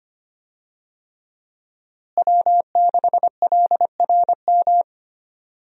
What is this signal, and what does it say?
Morse code, heard as a keyed tone, about 25 wpm, 700 Hz. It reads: W6LRM
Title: USA Call Signs - 5 Characters - Repeated 3x - 25wpm